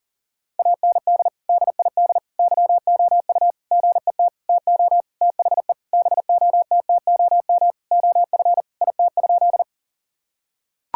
Morse code, heard as a keyed tone, about 39 wpm, 700 Hz. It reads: AND DID YOU GET TO THE BOTTOM OF IT?
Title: Sentences from Top 1000 Words - Review Part 6 - 40wpm